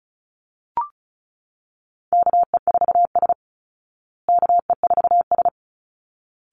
Morse code, heard as a keyed tone, about 34 wpm, 700 Hz. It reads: KE4S KE4S
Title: Call Signs Repeated 3x - 35wpm